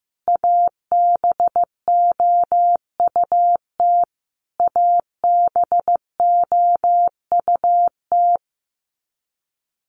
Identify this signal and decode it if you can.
Morse code, heard as a keyed tone, about 15 wpm, 700 Hz. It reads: ABOUT ABOUT
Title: Top 500 Words Repeated 15wpm